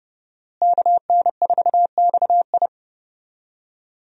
Morse code, heard as a keyed tone, about 32 wpm, 700 Hz. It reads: KN4XI
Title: USA Call Signs - 5 Characters - Repeated 3x - 30wpm